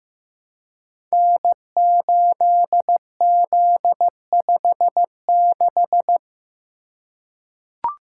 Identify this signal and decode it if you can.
Morse code, heard as a keyed tone, about 15 wpm, 700 Hz. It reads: N8Z56
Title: Code Groups - 5 Letter:Numbers - 15wpm